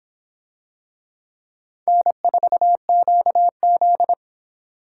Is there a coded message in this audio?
Yes: Morse code from 1.9 to 4.1 s, heard as a keyed tone, about 26 wpm, 700 Hz. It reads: N4QZ